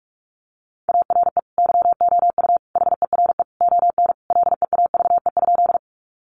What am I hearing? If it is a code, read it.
Morse code, heard as a keyed tone, about 44 wpm, 700 Hz. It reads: ARE YOU HERE ON LEAVE?
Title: Sentences from Top 400 Words 45wpm